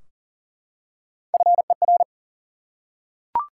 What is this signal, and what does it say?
Morse code, heard as a keyed tone, about 40 wpm, 700 Hz. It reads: FER
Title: Top 90 QSO Elements 40wpm